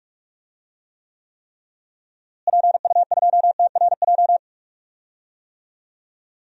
Morse code, heard as a keyed tone, about 47 wpm, 700 Hz. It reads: PU2TRJ
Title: DXCC-100 Call Signs - 6 Characters - Repeated 3x - 45wpm